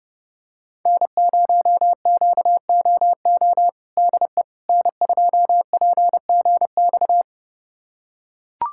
Morse code, heard as a keyed tone, about 29 wpm, 700 Hz. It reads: N0QOO DE N2PGX
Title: USA Callsign DE Callsign - 5 Characters - 30wpm